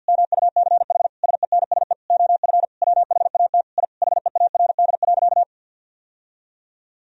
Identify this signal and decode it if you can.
Morse code, heard as a keyed tone, about 50 wpm, 700 Hz. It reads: MUCH SENSE OF WHAT I HEARD.
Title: Sentences from Top 1000 Words - Review Part 4 - 50wpm